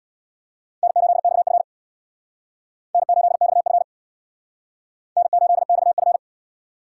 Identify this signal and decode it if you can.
Morse code, heard as a keyed tone, about 62 wpm, 700 Hz. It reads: N8XF N8XF N8XF
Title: Speed Racing - USA Call Signs 50wpm